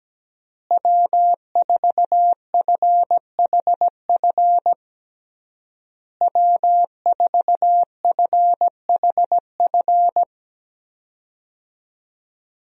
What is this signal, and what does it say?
Morse code, heard as a keyed tone, about 17 wpm, 700 Hz. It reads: W4FHF W4FHF
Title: USA Call Signs - 5 Characters - Repeated 3x - 17wpm